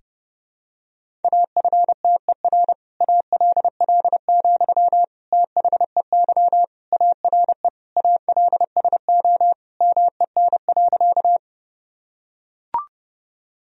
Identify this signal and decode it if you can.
Morse code, heard as a keyed tone, about 32 wpm, 700 Hz. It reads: AFTER ALL, THEY ARE ALSO MEN.
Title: Sentences from Top 1000 Words - Review Part 1 - 30wpm